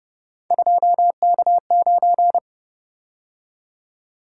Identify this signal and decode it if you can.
Morse code, heard as a keyed tone, about 30 wpm, 700 Hz. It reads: 2K9
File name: Speed-Racing - Three Letter-Number Combinations - 25wpm